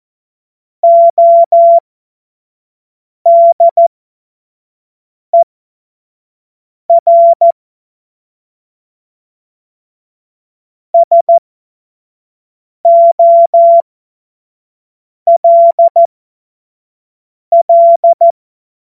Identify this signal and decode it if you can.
Morse code, heard as a keyed tone, about 14 wpm, spaced as for 5 wpm, 680 Hz. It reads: ODER SOLL
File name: Morsekurs / Witze / Witz 05 Dezember 2023 25 F70